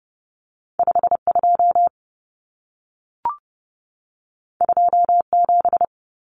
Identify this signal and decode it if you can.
Morse code, heard as a keyed tone, about 30 wpm, 700 Hz. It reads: 52 27
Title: Numbers - 2-Digit 30wpm